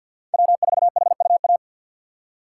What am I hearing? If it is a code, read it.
Morse code, heard as a keyed tone, about 51 wpm, 700 Hz. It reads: W4HUA
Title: USA Call Signs - 5 Characters - 4x Character Spacing - 50wpm